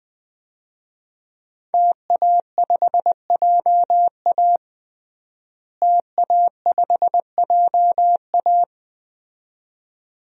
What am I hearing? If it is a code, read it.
Morse code, heard as a keyed tone, about 20 wpm, 700 Hz. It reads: TA5JA TA5JA
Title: DXCC-100 Call Signs - 5 Characters - Repeated 2x - 20wpm